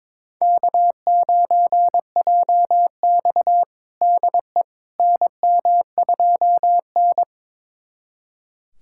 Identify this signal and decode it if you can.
Morse code, heard as a keyed tone, about 22 wpm, 700 Hz. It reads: K9JX DE NM2N
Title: USA Callsign DE Callsign - 4 Characters - 22wpm